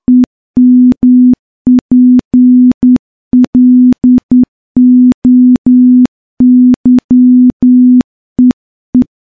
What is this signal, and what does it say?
Morse code, heard as a keyed tone, about 10 wpm, 250 Hz. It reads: EMPLOYEE
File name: morse-training-10WPM